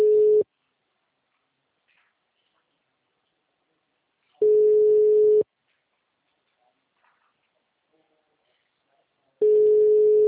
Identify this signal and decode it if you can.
Ringback tone (Europe, 425 Hz).
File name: out- -2010- - - .119